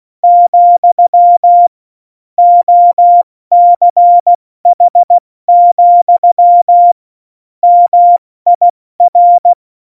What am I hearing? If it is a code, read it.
Morse code, heard as a keyed tone, about 16 wpm, spaced as for 14 wpm, 700 Hz. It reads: , OCH, MIR
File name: Morsekurs / Witze / Witz 06. Oktober 2020, 70 F80